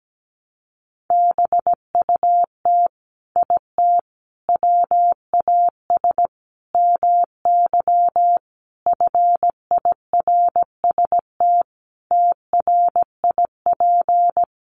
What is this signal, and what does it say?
Morse code, heard as a keyed tone, about 17 wpm, 700 Hz. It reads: BUT IT WAS MY FIRST TRIP
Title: Sentences from Top 1300 Words 17wpm